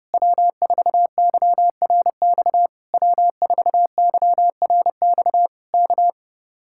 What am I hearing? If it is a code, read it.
Morse code, heard as a keyed tone, about 31 wpm, 700 Hz. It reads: W4YRX W4YRX K